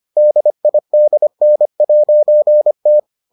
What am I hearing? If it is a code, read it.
Morse code, heard as a keyed tone, about 25 wpm, 600 Hz. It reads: DIDN'T